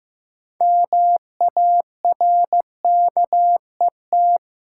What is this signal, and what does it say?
Morse code, heard as a keyed tone, about 15 wpm, 700 Hz. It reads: MARKET
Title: Sentences from Top 600 Words - New Word Drill - 15wpm